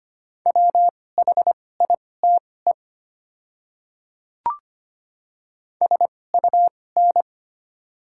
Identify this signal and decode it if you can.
Morse code, heard as a keyed tone, about 25 wpm, spaced as for 18 wpm, 700 Hz. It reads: WHITE SUN
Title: 2x Character Spacing - Top 500 Words - 25wpm